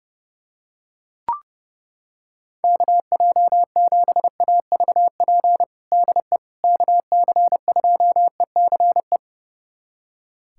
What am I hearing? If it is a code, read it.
Morse code, heard as a keyed tone, about 30 wpm, 700 Hz. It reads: KJ7AVP DE KC2ECE
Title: USA Callsign DE Callsign 30wpm